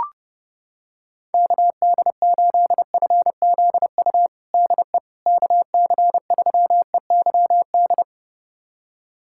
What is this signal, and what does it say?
Morse code, heard as a keyed tone, about 32 wpm, 700 Hz. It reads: KD8FZU DE KC3EYD
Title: USA Callsign DE Callsign 30wpm